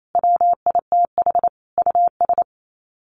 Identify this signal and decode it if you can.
Morse code, heard as a keyed tone, about 28 wpm, 700 Hz. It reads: WITH US
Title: Sets of 5 Words Encore 28wpm